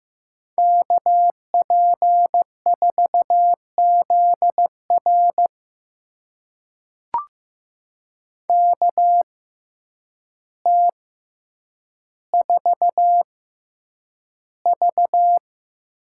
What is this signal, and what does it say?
Morse code, heard as a keyed tone, about 15 wpm, spaced as for 8 wpm, 700 Hz. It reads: KP4ZR KT4V
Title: USA Call Signs - 5 Characters - 6x Character Spacing - 15wpm